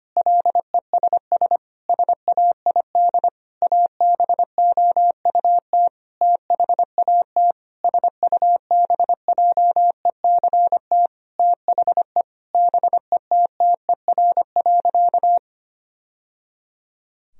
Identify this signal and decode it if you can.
Morse code, heard as a keyed tone, about 25 wpm, 700 Hz. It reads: LESS SAID ABOUT THAT SUBJECT THE BETTER.
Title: Sentences from Top 1000 Words - Review Part 2 - 25wpm